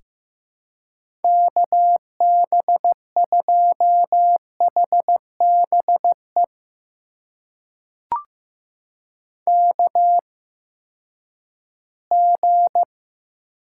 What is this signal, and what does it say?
Morse code, heard as a keyed tone, about 15 wpm, spaced as for 10 wpm, 700 Hz. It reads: KB2HBE KG
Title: USA Call Signs - 6 Characters - 8x Character Spacing - 15wpm